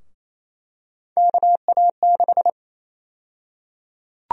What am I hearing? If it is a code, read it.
Morse code, heard as a keyed tone, about 28 wpm, 700 Hz. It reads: KA6
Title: USA Call Sign Prefixes - Repeated 2x - 28wpm